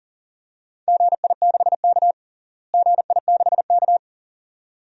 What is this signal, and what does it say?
Morse code, heard as a keyed tone, about 40 wpm, 700 Hz. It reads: GI6K GI6K
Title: DXCC-100 Call Signs - 4 Characters - Repeated 2x - 40wpm